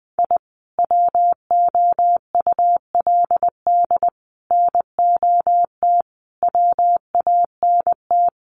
Morse code, heard as a keyed tone, about 20 wpm, 700 Hz. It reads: I WOULD NOT WANT
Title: Sets of 5 Words 20wpm